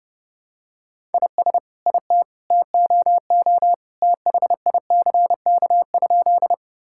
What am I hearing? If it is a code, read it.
Morse code, heard as a keyed tone, about 32 wpm, 700 Hz. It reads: IS IT TOO THICK?
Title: Sentences from Top 1600 Words 30wpm